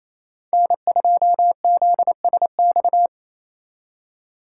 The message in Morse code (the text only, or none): N2ZSX